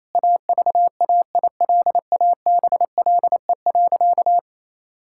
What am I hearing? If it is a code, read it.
Morse code, heard as a keyed tone, about 28 wpm, 700 Hz. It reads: AVAILABLE.
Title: Sentences from Top 1000 Words - Review Part 6 - 28wpm